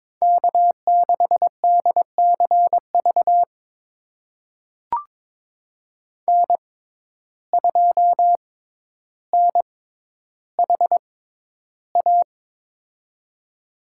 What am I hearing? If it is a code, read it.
Morse code, heard as a keyed tone, about 22 wpm, spaced as for 11 wpm, 700 Hz. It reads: K6DCV N2NHA
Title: USA Call Signs - 5 Characters - 6x Character Spacing - 22wpm